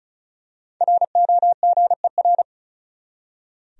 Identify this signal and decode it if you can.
Morse code, heard as a keyed tone, about 34 wpm, 700 Hz. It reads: ROGER
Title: USA Common Names - 35wpm